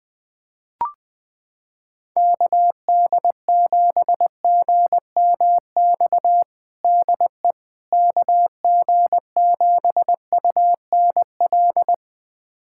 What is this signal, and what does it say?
Morse code, heard as a keyed tone, about 20 wpm, 700 Hz. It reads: KD7GMX DE KG7UNL
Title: USA Callsign DE Callsign - 6 Characters - 20wpm